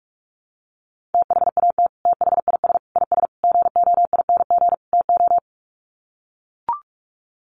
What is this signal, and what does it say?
Morse code, heard as a keyed tone, about 44 wpm, 700 Hz. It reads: THAT THIS IS GOING TO
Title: Sets of 5 Words Encore2 45wpm